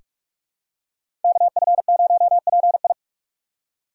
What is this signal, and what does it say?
Morse code, heard as a keyed tone, about 43 wpm, 700 Hz. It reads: KF0PI
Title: USA Call Signs - 5 Characters - 3x Character Spacing - 45wpm